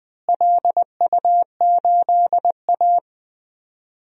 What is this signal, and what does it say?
Morse code, heard as a keyed tone, about 20 wpm, 700 Hz. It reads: LU8A
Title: DXCC-100 Call Signs - 4 Characters - Repeated 3x - 20wpm